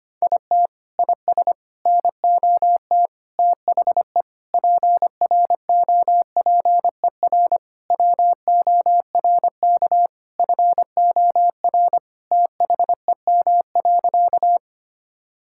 Morse code, heard as a keyed tone, about 25 wpm, 700 Hz. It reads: IT IS NOT THE PROPER WORK FOR THEM.
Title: Sentences from Top 1000 Words - Review Part 5 - 25wpm